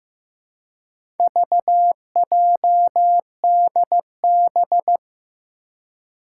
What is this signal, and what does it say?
Morse code, heard as a keyed tone, about 16 wpm, 700 Hz. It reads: VJDB
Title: Four Challenging Letter-Number Combinations 15wpm